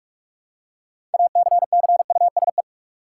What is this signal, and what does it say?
Morse code, heard as a keyed tone, about 44 wpm, 700 Hz. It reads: ACCUSE
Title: Six Letter Words - 45wpm